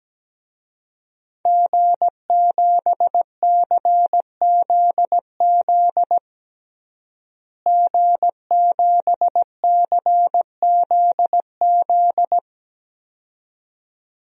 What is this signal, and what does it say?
Morse code, heard as a keyed tone, about 17 wpm, 700 Hz. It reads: G7CZZ G7CZZ
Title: DXCC-100 Call Signs - 5 Characters - Repeated 2x - 17wpm